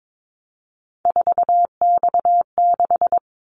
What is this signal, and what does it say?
Morse code, heard as a keyed tone, about 22 wpm, 700 Hz. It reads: 4X6
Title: Callsign Prefixes for Top 50 DXCC - 22wpm